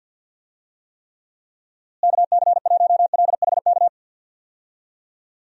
Morse code, heard as a keyed tone, about 46 wpm, 700 Hz. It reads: KC1LHK
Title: USA Call Signs - 6 Characters - Repeated 3x - 50wpm